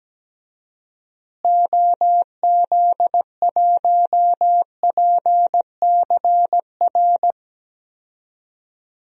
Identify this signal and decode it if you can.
Morse code, heard as a keyed tone, about 17 wpm, 700 Hz. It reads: OZ1PCR
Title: DXCC-100 Call Signs - 6 Characters - Repeated 2x - 17wpm